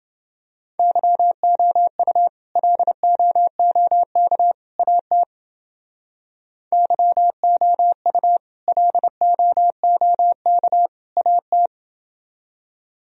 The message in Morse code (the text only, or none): YOU LOOK AT YOU LOOK AT